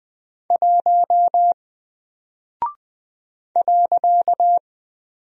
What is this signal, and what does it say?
Morse code, heard as a keyed tone, about 20 wpm, 700 Hz. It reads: 1 .
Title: Single Character (Letter:Number:Punctuation:Prosign) Alt 20wpm